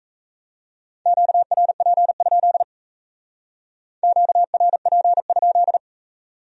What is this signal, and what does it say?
Morse code, heard as a keyed tone, about 33 wpm, 700 Hz. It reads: QRP? QRP?
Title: Speed-Racing Q-Codes - 28wpm